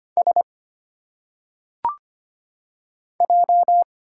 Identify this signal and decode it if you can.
Morse code, heard as a keyed tone, about 25 wpm, 700 Hz. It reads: S J
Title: Single Letters - Phonetic Pronunciation 25wpm